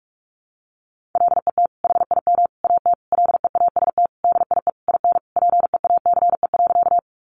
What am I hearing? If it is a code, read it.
Morse code, heard as a keyed tone, about 44 wpm, 700 Hz. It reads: LET HIM AT LEAST DIE IN PEACE.
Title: Sentences from Top 1500 Words 45wpm